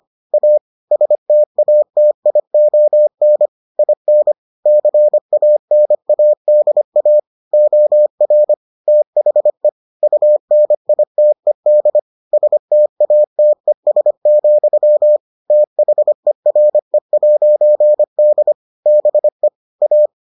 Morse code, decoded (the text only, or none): A STATION IN CANADA OR THE UNITED STATES, THERE'D BE A